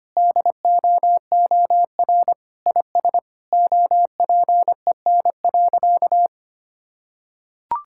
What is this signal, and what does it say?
Morse code, heard as a keyed tone, about 25 wpm, 700 Hz. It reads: DOOR IS OPEN.
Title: Sentences from Top 1000 Words - Review Part 6 - 25wpm